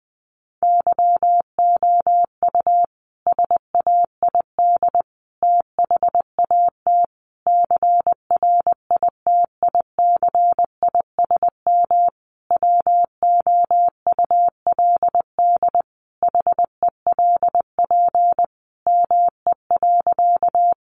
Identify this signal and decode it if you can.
Morse code, heard as a keyed tone, about 19 wpm, 700 Hz. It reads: YOU SAID THAT CRITICISM WOULD HELP ME.
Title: Sentences from Top 1300 Words 20wpm